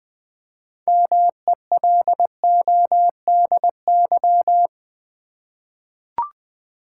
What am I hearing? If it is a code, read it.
Morse code, heard as a keyed tone, about 20 wpm, 700 Hz. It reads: MELODY
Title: 6 Letter Words - First 3 Letters Given - 20wpm